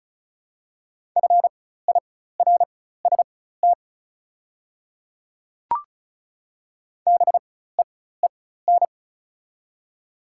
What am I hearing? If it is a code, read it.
Morse code, heard as a keyed tone, about 34 wpm, spaced as for 16 wpm, 700 Hz. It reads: FIRST BEEN
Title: 4x Character Spacing - Top 100 Words - 35wpm